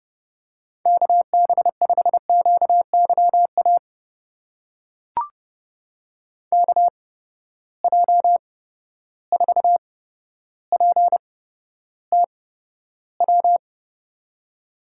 Morse code, heard as a keyed tone, about 30 wpm, spaced as for 13 wpm, 700 Hz. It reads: KB5QYA KJ4PTW
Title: USA Call Signs - 6 Characters - 8x Character Spacing - 30wpm